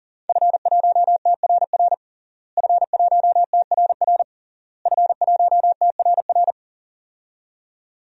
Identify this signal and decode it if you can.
Morse code, heard as a keyed tone, about 40 wpm, 700 Hz. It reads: F1TRR F1TRR F1TRR
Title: DXCC-100 Call Signs - 5 Characters - Repeated 3x - 40wpm